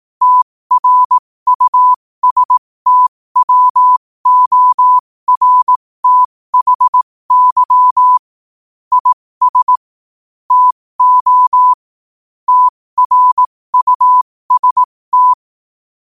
Morse code, heard as a keyed tone, about 18 wpm, spaced as for 15 wpm, 1000 Hz. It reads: TRUSTWORTHY IS TO TRUST
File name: qotdfiles / 15wpm